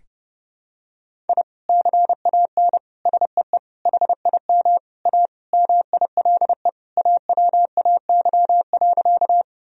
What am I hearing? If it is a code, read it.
Morse code, heard as a keyed tone, about 30 wpm, 700 Hz. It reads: I CAN SEE HIM A MILE AWAY.
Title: Sentences from Top 1000 Words - Review Part 1 - 30wpm